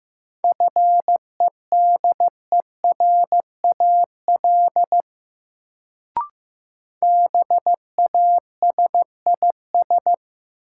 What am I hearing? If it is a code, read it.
Morse code, heard as a keyed tone, about 15 wpm, 700 Hz. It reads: FEDERAL BASIS
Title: Sets of 3 Words - New Word Drill - 15wpm